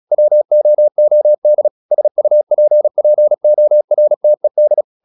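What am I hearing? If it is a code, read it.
Morse code, heard as a keyed tone, about 36 wpm, 600 Hz. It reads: WOOD SUPPORTED
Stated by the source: Morse practice at 36 wpm, text of Ken Follett's novel Never, from CW med Gnister.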